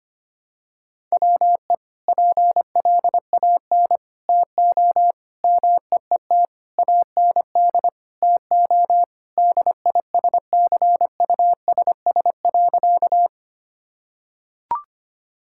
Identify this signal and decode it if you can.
Morse code, heard as a keyed tone, about 25 wpm, 700 Hz. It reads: WE PLAN TO MEET AND TO DISCUSS.
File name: practice / 25wpm / Sentences with 2-Word Phrases - 25wpm